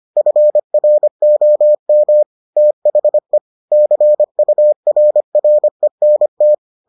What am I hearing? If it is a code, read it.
Morse code, heard as a keyed tone, about 25 wpm, 600 Hz. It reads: FROM THE CURRENT